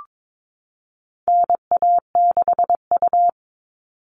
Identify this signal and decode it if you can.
Morse code, heard as a keyed tone, about 22 wpm, 700 Hz. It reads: NA6U
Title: All Cut Numbers - 4-Digit 22wpm